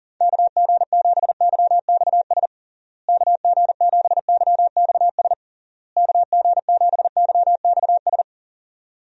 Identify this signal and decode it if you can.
Morse code, heard as a keyed tone, about 37 wpm, 700 Hz. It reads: KG7YXS KG7YXS KG7YXS
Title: USA Call Signs - 6 Characters - Repeated 3x - 40wpm